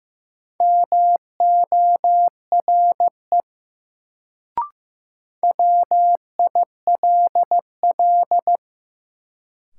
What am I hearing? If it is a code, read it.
Morse code, heard as a keyed tone, about 15 wpm, 700 Hz. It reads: MORE WILL
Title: Sentences from Top 100 Words - New Word Drill - 15wpm